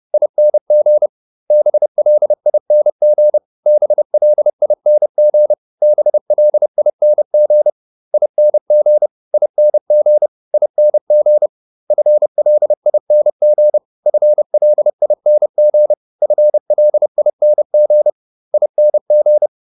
Morse code, heard as a keyed tone, about 30 wpm, 600 Hz. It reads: ING BLING BLING BLING ING ING ING FLING FLING FLING ING